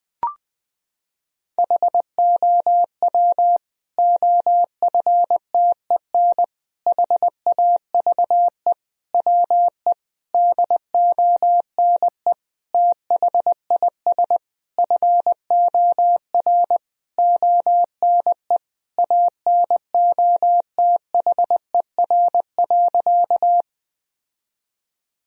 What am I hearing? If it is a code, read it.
Morse code, heard as a keyed tone, about 20 wpm, 700 Hz. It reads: HOW OFTEN HAVE WE DONE THIS FOR ONE ANOTHER.
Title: Sentences from Top 1000 Words - Review Part 1 - 20wpm